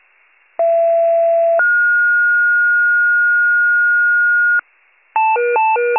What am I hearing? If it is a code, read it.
Two-tone page, heard as a one-second tone, then a three-second tone, about 670 then 1430 Hz. It is heard from 0.6 to 4.6 s.